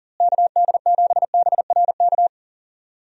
Morse code, heard as a keyed tone, about 40 wpm, 700 Hz. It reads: KD7BRK
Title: USA Call Signs - 6 Characters - Repeated 3x - 40wpm